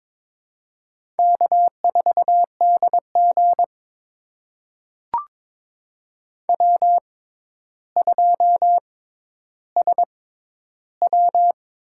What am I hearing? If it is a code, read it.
Morse code, heard as a keyed tone, about 21 wpm, spaced as for 11 wpm, 700 Hz. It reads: K4DG W2SW
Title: USA Call Signs - 6x Character Spacing - 22wpm